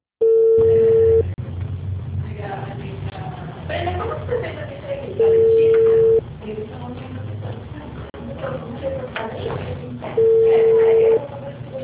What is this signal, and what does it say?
Ringback tone (China, 450 Hz), with speech.